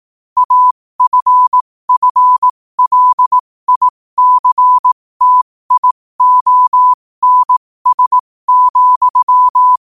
Morse code, heard as a keyed tone, about 18 wpm, spaced as for 15 wpm, 1000 Hz. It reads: AFFLICTIONS,
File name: qotdfiles / 15wpm